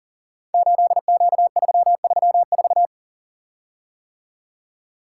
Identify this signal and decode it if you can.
Morse code, heard as a keyed tone, about 42 wpm, 700 Hz. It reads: 8Q334
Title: Code Groups - 5 Letter:Numbers - 40wpm